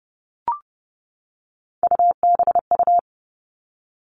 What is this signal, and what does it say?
Morse code, heard as a keyed tone, about 28 wpm, 700 Hz. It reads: UBU
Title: All Cut Numbers - 3-Digit 30wpm